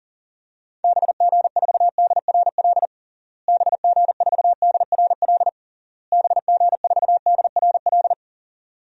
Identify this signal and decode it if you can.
Morse code, heard as a keyed tone, about 40 wpm, 700 Hz. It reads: BG4DRL BG4DRL BG4DRL
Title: DXCC-100 Call Signs - 6 Characters - Repeated 3x - 40wpm